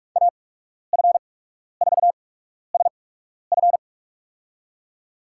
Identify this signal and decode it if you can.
Morse code, heard as a keyed tone, about 44 wpm, spaced as for 12 wpm, 690 Hz. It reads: AF4SF
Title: USA Call Signs - 5 Characters - 8x Character Spacing - 45wpm